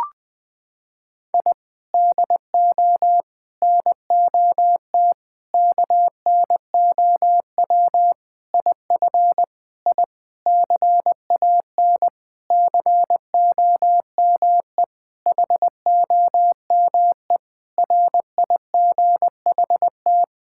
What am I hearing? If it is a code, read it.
Morse code, heard as a keyed tone, about 20 wpm, 700 Hz. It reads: I DO NOT KNOW IF I CAN COME HOME RIGHT
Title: Sentences with 5-Word Phrases - 20wpm